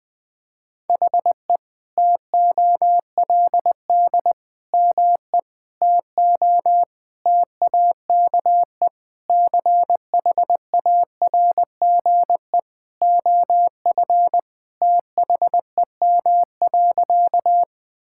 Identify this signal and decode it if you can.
Morse code, heard as a keyed tone, about 20 wpm, 700 Hz. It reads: HE TOLD ME TO TAKE CHARGE OF THEM.
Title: Sentences from Top 1000 Words - Review Part 6 - 20wpm